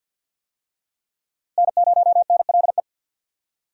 Morse code, heard as a keyed tone, about 47 wpm, 700 Hz. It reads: N0NLE
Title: USA Call Signs - 5 Characters - Repeated 3x - 50wpm